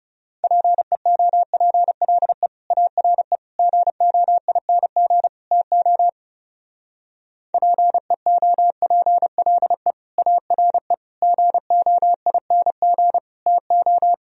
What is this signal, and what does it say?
Morse code, heard as a keyed tone, about 34 wpm, 700 Hz. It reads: PEOPLE ARE GOING TO PEOPLE ARE GOING TO
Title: Speed-Racing - Sets of 4 Words 30wpm